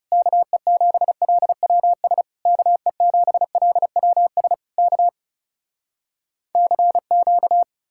Morse code, heard as a keyed tone, about 38 wpm, 700 Hz. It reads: KE7LWS KE7LWS K CQ